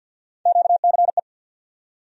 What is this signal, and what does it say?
Morse code, heard as a keyed tone, about 49 wpm, 700 Hz. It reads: QCE
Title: USA Call Sign Postfixes - 3 Letters 50wpm